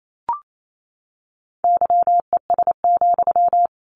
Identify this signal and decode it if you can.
Morse code, heard as a keyed tone, about 28 wpm, 700 Hz. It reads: YES,
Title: Sentences from Top 1100 Words 28wpm